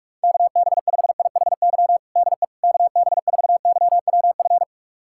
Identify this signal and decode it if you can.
Morse code, heard as a keyed tone, about 46 wpm, 700 Hz. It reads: KB5IHY DE KB4YWF
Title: USA Callsign DE Callsign - 6 Characters - 45wpm